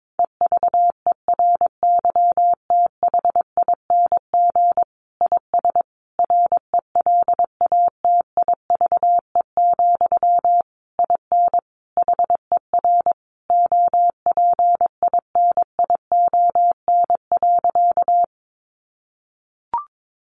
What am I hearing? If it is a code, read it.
Morse code, heard as a keyed tone, about 22 wpm, 700 Hz. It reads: EVERYTHING IS RELATIVE, IN HER OPINION.
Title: Sentences from Top 1400 Words 22wpm